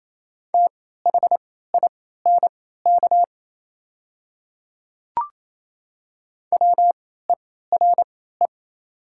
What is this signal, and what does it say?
Morse code, heard as a keyed tone, about 28 wpm, spaced as for 16 wpm, 700 Hz. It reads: THINK WERE
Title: 3x Character Spacing - Top 200 Words - 28wpm